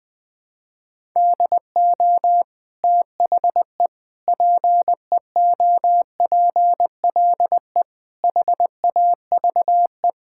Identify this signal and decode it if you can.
Morse code, heard as a keyed tone, about 20 wpm, 700 Hz. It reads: DO THE PEOPLE HAVE
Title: Sentences from Top 1000 Words - Review Part 5 - 20wpm